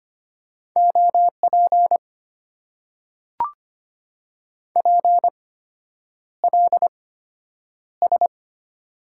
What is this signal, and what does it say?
Morse code, heard as a keyed tone, about 25 wpm, spaced as for 9 wpm, 700 Hz. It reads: OP PLS
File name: Top 90 QSO Elements - 8x Character Spacing - 25wpm